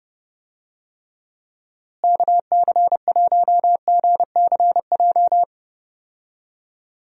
Morse code, heard as a keyed tone, about 30 wpm, 700 Hz. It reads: KC1GCJ
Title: USA Call Signs - 6 Characters - Repeated 3x - 30wpm